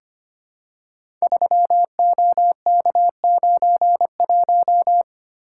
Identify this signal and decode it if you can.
Morse code, heard as a keyed tone, about 25 wpm, 700 Hz. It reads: 3OK91
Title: Code Groups - 5 Letter:Numbers - 25wpm